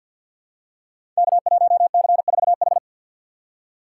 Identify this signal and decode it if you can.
Morse code, heard as a keyed tone, about 51 wpm, 690 Hz. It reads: K1C4H
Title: Code Groups - 5 Letter:Numbers - 50wpm